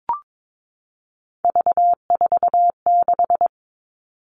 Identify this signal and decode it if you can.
Morse code, heard as a keyed tone, about 22 wpm, 700 Hz. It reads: V46
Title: All Cut Numbers - 3-Digit 22wpm